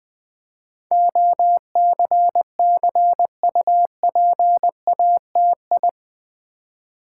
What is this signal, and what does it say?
Morse code, heard as a keyed tone, about 21 wpm, 700 Hz. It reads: OCCUPATI
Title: Build A Word - 4 Letters At A Time - 10-Letter Words - 20wpm